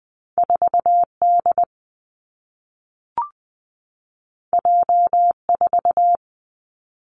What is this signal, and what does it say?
Morse code, heard as a keyed tone, about 20 wpm, 700 Hz. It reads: 4 D J 4
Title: Two Challenging Letter-Number Combinations 20wpm